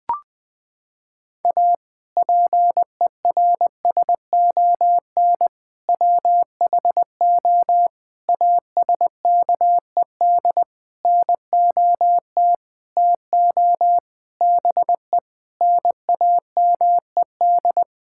Morse code, heard as a keyed tone, about 20 wpm, 700 Hz. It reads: A PERSON WHO ASKED NOT TO BE NAMED
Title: Sentences with 6-Word Phrases - 20wpm